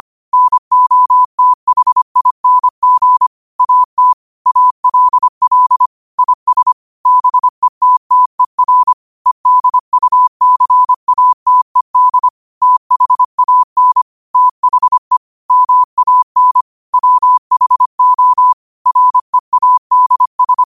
Morse code, heard as a keyed tone, about 25 wpm, 1000 Hz. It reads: NOTHING AT ALL IS BETTER EDUCATED THAN THE MAN WHO READS
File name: qotdfiles / 25wpm